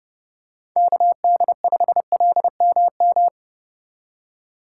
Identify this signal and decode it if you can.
Morse code, heard as a keyed tone, about 29 wpm, 700 Hz. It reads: KD5LMM